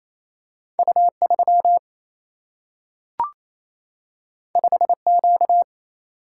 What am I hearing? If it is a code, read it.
Morse code, heard as a keyed tone, about 28 wpm, 700 Hz. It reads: U 3 5 Q
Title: Single-Letter Single-Number 28wpm